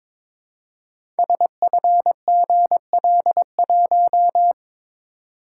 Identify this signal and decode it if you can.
Morse code, heard as a keyed tone, about 22 wpm, 700 Hz. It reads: SFGL1
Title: Code Groups - 5 Letter:Numbers - 22wpm